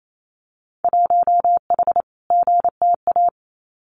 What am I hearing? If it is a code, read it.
Morse code, heard as a keyed tone, about 28 wpm, 700 Hz. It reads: 1H GTA
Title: Winter Field Day Exchanges Encore 28wpm